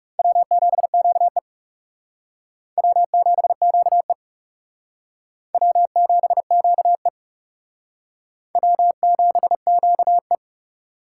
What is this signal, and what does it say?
Morse code, heard as a keyed tone, about 38 wpm, 700 Hz. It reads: W7QE W7QE W7QE W7QE